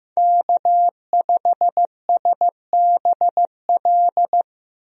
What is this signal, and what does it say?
Morse code, heard as a keyed tone, about 15 wpm, 700 Hz. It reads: K5SBL
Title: USA Call Signs - 5 Characters - Repeated 2x - 15wpm